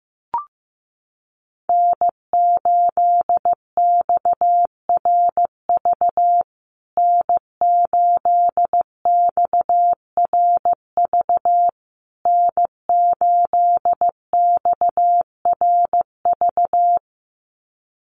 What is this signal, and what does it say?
Morse code, heard as a keyed tone, about 15 wpm, 700 Hz. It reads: N8XRV N8XRV N8XRV
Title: USA Call Signs Repeated 3x 15wpm